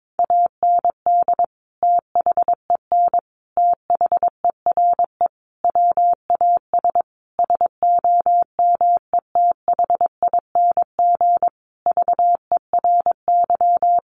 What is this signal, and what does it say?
Morse code, heard as a keyed tone, about 22 wpm, 700 Hz. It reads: AND THEN THERE WAS SOMETHING VERY
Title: Sentences from Top 1400 Words 22wpm